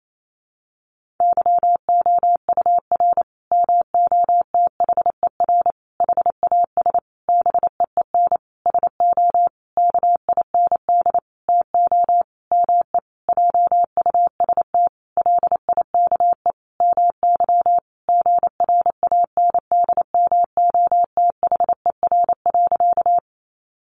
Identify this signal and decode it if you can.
Morse code, heard as a keyed tone, about 28 wpm, 700 Hz. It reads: YOUR MOTHER HAS BEEN SO KIND TO ME JUST LIKE MY GRANDMOTHER.
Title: Sentences from Top 1200 Words 28wpm